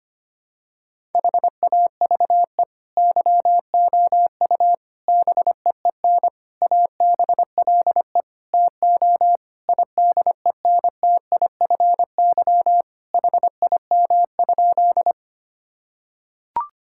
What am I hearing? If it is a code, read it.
Morse code, heard as a keyed tone, about 25 wpm, 700 Hz. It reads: HAVE YOU BEEN ABLE TO IDENTIFY HIM?
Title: Sentences from Top 1000 Words - Review Part 6 - 25wpm